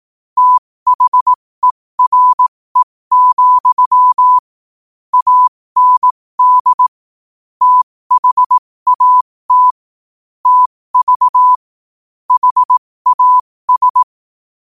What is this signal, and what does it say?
Morse code, heard as a keyed tone, about 18 wpm, spaced as for 15 wpm, 1000 Hz. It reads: THERE, AND THAT TV HAS
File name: qotdfiles / 15wpm